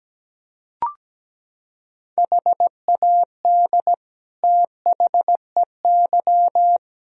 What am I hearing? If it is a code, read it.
Morse code, heard as a keyed tone, about 17 wpm, 700 Hz. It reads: HAD THEY
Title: Sentences from Top 1000 Words - Review Part 2 - 17wpm